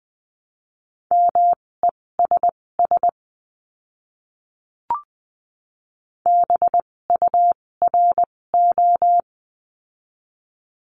Morse code, heard as a keyed tone, about 20 wpm, spaced as for 16 wpm, 700 Hz. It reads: MESS BURO
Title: 256 Lesson 42a - Words 12-20wpm